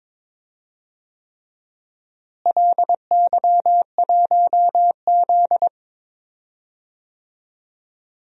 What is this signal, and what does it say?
Morse code, heard as a keyed tone, about 22 wpm, 700 Hz. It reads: LY1Z
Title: DXCC-100 Call Signs - 4 Characters - Repeated 3x - 22wpm